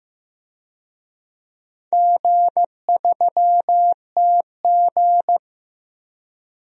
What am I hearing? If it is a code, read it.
Morse code, heard as a keyed tone, about 15 wpm, 700 Hz. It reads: G3TG
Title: DXCC-100 Call Signs - 4 Characters - 15wpm